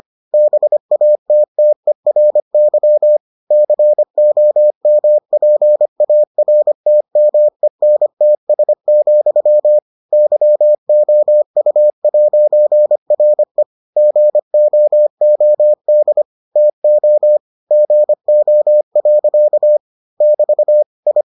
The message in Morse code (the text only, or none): BATTERY COMPARTMENTS, YOU'RE GOOD TO GO. <BT> I